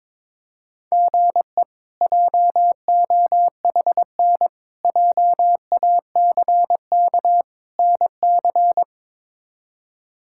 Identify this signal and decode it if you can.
Morse code, heard as a keyed tone, about 22 wpm, 700 Hz. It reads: GE JOHN JACK NC
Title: K1USN SST QSO Practice - 22wpm